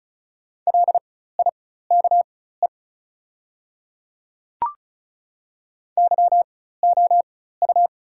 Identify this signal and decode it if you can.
Morse code, heard as a keyed tone, about 34 wpm, spaced as for 16 wpm, 700 Hz. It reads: LIKE YOU
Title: 4x Character Spacing - Top 100 Words - 35wpm